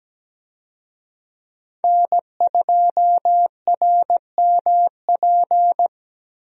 Morse code, heard as a keyed tone, about 17 wpm, 700 Hz. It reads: N2RMP
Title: USA Call Signs - 5 Characters - 17wpm